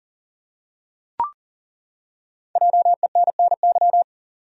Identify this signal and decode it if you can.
Morse code, heard as a keyed tone, about 40 wpm, 700 Hz. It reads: JENNY
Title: USA Common Names - Max 5 Letters - 40wpm